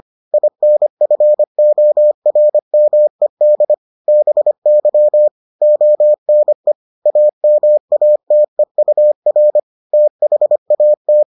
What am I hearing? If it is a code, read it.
Morse code, heard as a keyed tone, about 25 wpm, 600 Hz. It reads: INFORMED BY ONE AMATEUR THAT